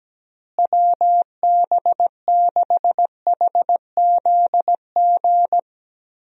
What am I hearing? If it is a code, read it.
Morse code, heard as a keyed tone, about 17 wpm, 700 Hz. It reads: WB6HZG
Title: USA Call Signs - 6 Characters - Repeated 3x - 17wpm